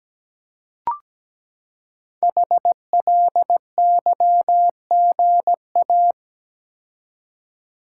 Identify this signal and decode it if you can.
Morse code, heard as a keyed tone, about 17 wpm, 700 Hz. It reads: HLYGA
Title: Code Groups - 5 Letters - 17wpm